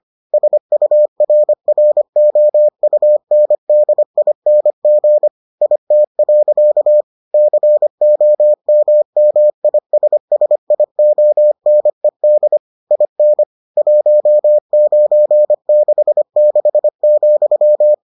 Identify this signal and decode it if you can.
Morse code, heard as a keyed tone, about 25 wpm, 600 Hz. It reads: SURROUNDING IT. COMMISSIONED IN 1966,